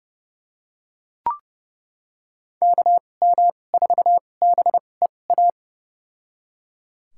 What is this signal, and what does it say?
Morse code, heard as a keyed tone, about 30 wpm, spaced as for 22 wpm, 700 Hz. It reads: KM4BEA
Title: USA Call Signs - 6 Characters - 2x Character Spacing - 30wpm